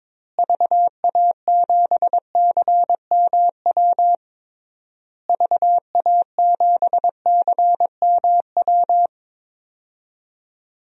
Morse code, heard as a keyed tone, about 22 wpm, 700 Hz. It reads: VA7CMW VA7CMW
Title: DXCC-100 Call Signs - 6 Characters - Repeated 3x - 22wpm